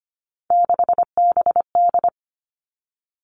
Morse code, heard as a keyed tone, about 25 wpm, 700 Hz. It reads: 6BD
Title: Three Challenging Letter-Number Combinations 25wpm